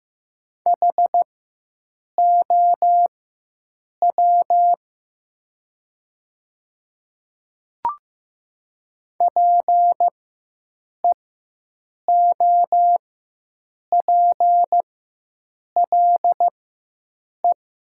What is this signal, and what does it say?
Morse code, heard as a keyed tone, about 15 wpm, spaced as for 7 wpm, 700 Hz. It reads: HOW PEOPLE
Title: 4x Character Spacing - Top 100 Words - 15wpm